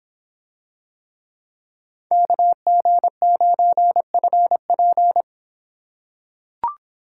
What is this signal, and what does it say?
Morse code, heard as a keyed tone, about 24 wpm, 700 Hz. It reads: KG9FP